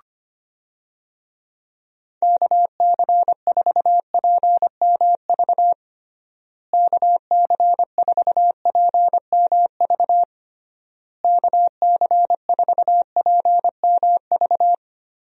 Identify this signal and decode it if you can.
Morse code, heard as a keyed tone, about 25 wpm, 700 Hz. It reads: KC4PMV KC4PMV KC4PMV
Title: USA Call Signs - 6 Characters - Repeated 3x - 25wpm